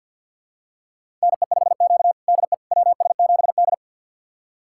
Morse code, heard as a keyed tone, about 47 wpm, 700 Hz. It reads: NE5Q DE WS7D
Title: USA Callsign DE Callsign - 4 Characters - 50wpm